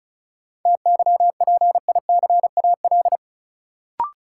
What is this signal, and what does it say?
Morse code, heard as a keyed tone, about 34 wpm, 700 Hz. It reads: TYPICAL
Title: Sentences from Top 1100 Words - New Word Drill - 35wpm